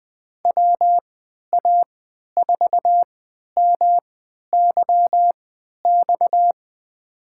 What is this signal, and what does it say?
Morse code, heard as a keyed tone, about 20 wpm, spaced as for 11 wpm, 700 Hz. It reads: WA4MYX
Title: USA Call Signs - 6 Characters - 3x Character Spacing - 20wpm